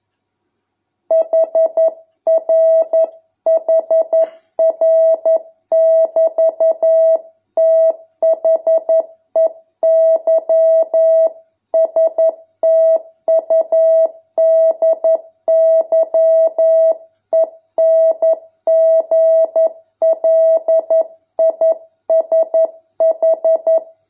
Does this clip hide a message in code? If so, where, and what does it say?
Yes: Morse code from 1.1 to 23.8 s, heard as a keyed tone, about 11 wpm, 650 Hz. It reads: HRHR<BT>THEY STUDY ENGLISH